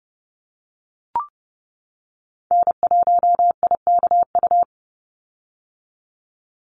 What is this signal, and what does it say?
Morse code, heard as a keyed tone, about 32 wpm, 700 Hz. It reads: N1IKU
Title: USA Call Signs 30wpm